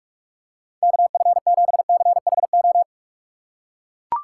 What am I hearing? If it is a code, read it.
Morse code, heard as a keyed tone, about 44 wpm, 700 Hz. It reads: KF7CHO
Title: USA Call Signs - 6x Character Spacing - 45wpm